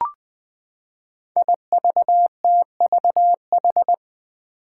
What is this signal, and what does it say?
Morse code, heard as a keyed tone, about 20 wpm, 700 Hz. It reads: IVTVH
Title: Five Challenging Letter-Number Combinations 20wpm